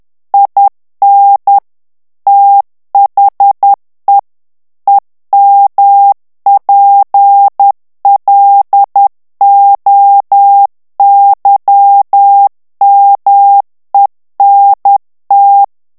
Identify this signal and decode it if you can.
Morse code, heard as a keyed tone, about 10 wpm, 800 Hz. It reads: IN THE EMPLOYMENT
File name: ANGLAIS texte clair 3 vitesse 10 mots minute